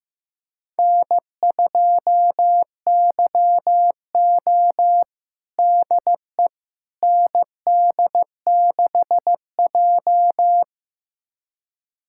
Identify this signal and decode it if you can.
Morse code, heard as a keyed tone, about 15 wpm, 700 Hz. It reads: N2YO DE ND6J
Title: USA Callsign DE Callsign - 4 Characters - 15wpm